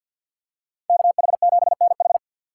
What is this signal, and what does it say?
Morse code, heard as a keyed tone, about 53 wpm, 700 Hz. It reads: KH7NH
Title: USA Call Signs - 5 Characters - 2x Character Spacing - 50wpm